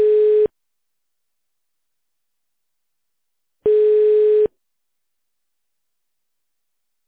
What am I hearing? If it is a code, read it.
Ringback tone (425 Hz, 1 s on, 3 s off).